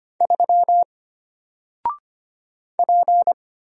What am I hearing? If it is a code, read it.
Morse code, heard as a keyed tone, about 25 wpm, 700 Hz. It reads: 3 P
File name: Single-Letter-Number-25wpm